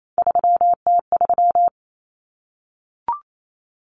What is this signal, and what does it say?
Morse code, heard as a keyed tone, about 28 wpm, 700 Hz. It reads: 3T3
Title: Common Cut Numbers - 3-Digit 28wpm